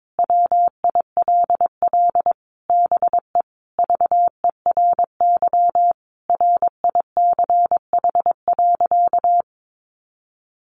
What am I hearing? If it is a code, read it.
Morse code, heard as a keyed tone, about 22 wpm, 700 Hz. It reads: WILL BE VERY RICH.
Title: Sentences from Top 1200 Words 22wpm